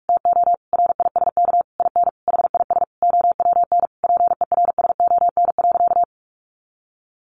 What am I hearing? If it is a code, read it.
Morse code, heard as a keyed tone, about 44 wpm, 700 Hz. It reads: TO RISK IN HIS OWN PERSON.
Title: Sentences from Top 600 Words 45wpm